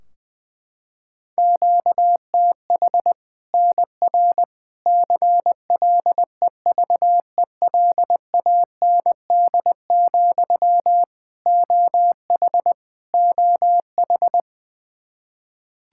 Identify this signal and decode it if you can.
Morse code, heard as a keyed tone, about 20 wpm, 700 Hz. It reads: QTH NR CLEVELAND, OH OH
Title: Practice QSO - Exchange 1 - Beginning - 20wpm